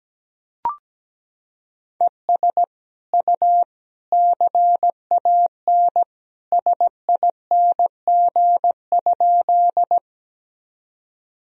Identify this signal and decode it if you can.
Morse code, heard as a keyed tone, about 17 wpm, 700 Hz. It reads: ES U CAN SING?
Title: Sentences from Top 400 Words and Abbreviations 17wpm